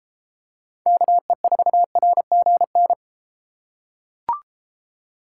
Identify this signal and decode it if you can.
Morse code, heard as a keyed tone, about 32 wpm, 700 Hz. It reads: KE4RGN